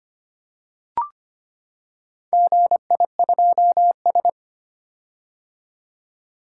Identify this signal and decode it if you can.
Morse code, heard as a keyed tone, about 25 wpm, 700 Hz. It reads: GI2S
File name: practice / 25wpm / Four Letter-Number Combinations - 25wpm